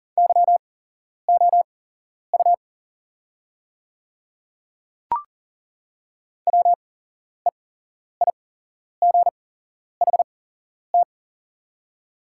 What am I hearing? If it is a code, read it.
Morse code, heard as a keyed tone, about 37 wpm, spaced as for 11 wpm, 690 Hz. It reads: YOU WEIGHT
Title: 8x Character Spacing - Top 500 Words - 40wpm